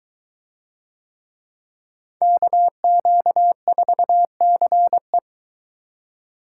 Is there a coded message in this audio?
Yes: Morse code from 2.2 to 5.2 s, heard as a keyed tone, about 22 wpm, 700 Hz. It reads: KQ4CE